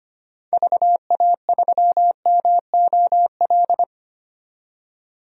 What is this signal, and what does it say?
Morse code, heard as a keyed tone, about 25 wpm, 700 Hz. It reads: VA3MOL
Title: DXCC-100 Call Signs - 6 Characters - 25wpm